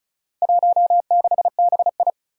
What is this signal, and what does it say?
Morse code, heard as a keyed tone, about 34 wpm, 700 Hz. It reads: 16BI
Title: Four Challenging Letter-Number Combinations 35wpm